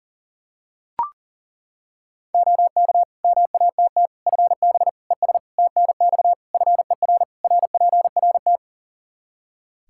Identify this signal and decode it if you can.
Morse code, heard as a keyed tone, about 40 wpm, 700 Hz. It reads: OK MATT FB ES TNX FER RPRT
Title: Practice QSO - Exchange 2 - Info - 40wpm